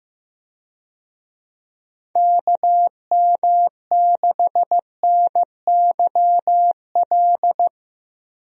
Morse code, heard as a keyed tone, about 15 wpm, 700 Hz. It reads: KM6NYL
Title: USA Call Signs - 6 Characters - 15wpm